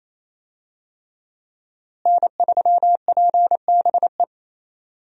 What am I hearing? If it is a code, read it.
Morse code, heard as a keyed tone, about 28 wpm, 700 Hz. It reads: N3PBE
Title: USA Call Signs - 5 Characters - Repeated 3x - 28wpm